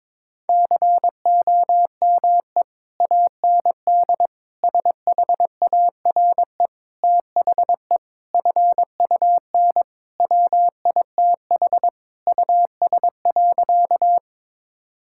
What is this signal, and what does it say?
Morse code, heard as a keyed tone, about 22 wpm, 700 Hz. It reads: COME AND SHARE THE FUN WITH US.
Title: Sentences from Top 1000 Words - Review Part 6 - 22wpm